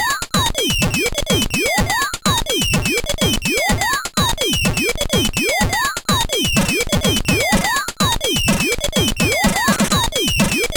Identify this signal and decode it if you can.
There is no code.